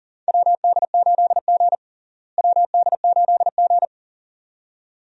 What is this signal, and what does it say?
Morse code, heard as a keyed tone, about 40 wpm, 700 Hz. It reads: WD8G WD8G
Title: USA Call Signs - 4 Characters - Repeated 2x - 40wpm